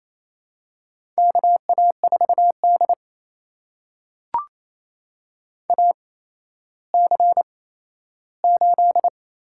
Morse code, heard as a keyed tone, about 27 wpm, spaced as for 13 wpm, 700 Hz. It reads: KA4D AC8
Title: USA Call Signs - 4 Characters - 8x Character Spacing - 28wpm